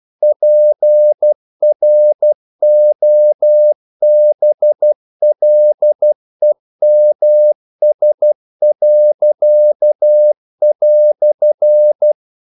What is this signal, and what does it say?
Morse code, heard as a keyed tone, about 12 wpm, 600 Hz. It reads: PROBLEMS."
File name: CW / Practice / EnglishQuotations0112